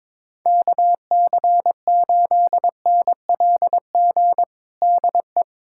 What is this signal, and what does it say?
Morse code, heard as a keyed tone, about 21 wpm, 700 Hz. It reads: KC8NLG DE
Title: USA Callsign DE Callsign - 6 Characters - 22wpm